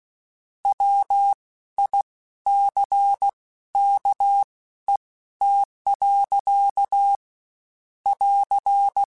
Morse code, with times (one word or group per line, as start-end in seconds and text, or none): 0.7-7.2 WICKET.
8.1-9.0 <AR>